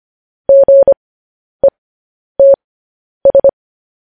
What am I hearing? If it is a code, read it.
Morse code, heard as a keyed tone, about 25 wpm, spaced as for 10 wpm, 550 Hz. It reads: GETS
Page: news headlines in morse code at 10 wpm